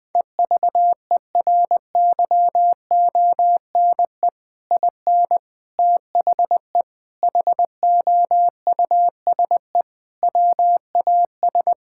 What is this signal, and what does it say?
Morse code, heard as a keyed tone, about 20 wpm, 700 Hz. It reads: EVERYONE IN THE HOUSE WAS
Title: Sentences with 3-Word Phrases - 20wpm